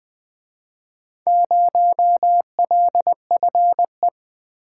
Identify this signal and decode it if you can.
Morse code, heard as a keyed tone, about 25 wpm, 700 Hz. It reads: 0LFE
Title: Speed-Racing - Four Letter-Number Combinations - 20wpm